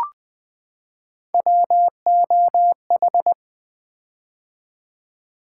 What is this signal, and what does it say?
Morse code, heard as a keyed tone, about 20 wpm, 700 Hz. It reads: WOH
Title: Three Letter-Number Combinations - 20wpm